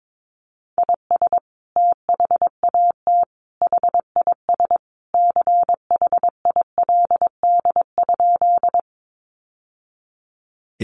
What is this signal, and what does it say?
Morse code, heard as a keyed tone, about 22 wpm, 700 Hz. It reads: IS THAT HIS CHILD?
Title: Sentences from Top 800 Words 22wpm